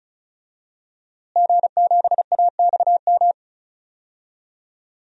Morse code, heard as a keyed tone, about 34 wpm, 700 Hz. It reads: G7AXM
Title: DXCC-100 Call Signs - 5 Characters - Repeated 2x - 35wpm